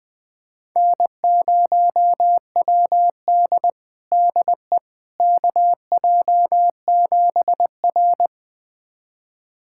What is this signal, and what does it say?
Morse code, heard as a keyed tone, about 20 wpm, 700 Hz. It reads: N0WD DE KJ7R
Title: USA Callsign DE Callsign - 4 Characters - 20wpm